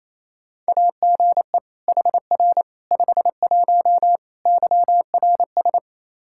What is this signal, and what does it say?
Morse code, heard as a keyed tone, about 28 wpm, 700 Hz. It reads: AGE HR 51 YRS
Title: Practice QSO - Exchange 3 - More Info - 28wpm